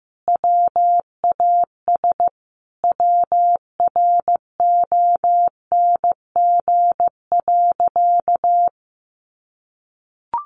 Morse code, heard as a keyed tone, about 15 wpm, 700 Hz. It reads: WAS WRONG.
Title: Sentences from Top 1300 Words 15wpm